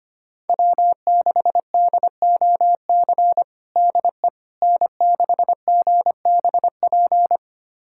Morse code, heard as a keyed tone, about 25 wpm, 700 Hz. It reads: W6DOC DE N6GBP
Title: USA Callsign DE Callsign - 5 Characters - 25wpm